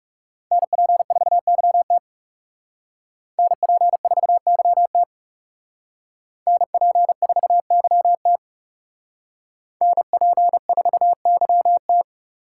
Morse code, heard as a keyed tone, about 35 wpm, 700 Hz. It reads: NP4YT NP4YT NP4YT NP4YT
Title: Speed Racing - USA 2x2 Call Signs 30wpm